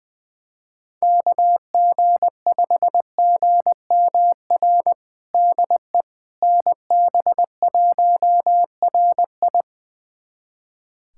Morse code, heard as a keyed tone, about 20 wpm, 700 Hz. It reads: KG5GMR DE NB1RI
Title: USA Callsign DE Callsign 20wpm